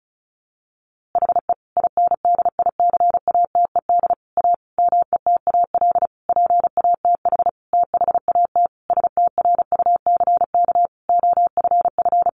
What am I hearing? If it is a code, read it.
Morse code, heard as a keyed tone, about 34 wpm, 700 Hz. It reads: HE INDICATED A METAL PATH THAT STRUCK OFF
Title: A Sound Of Thunder 35wpm